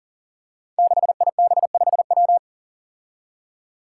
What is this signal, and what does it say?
Morse code, heard as a keyed tone, about 40 wpm, 700 Hz. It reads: 6IB5W
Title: Five Challenging Letter-Number Combinations 40wpm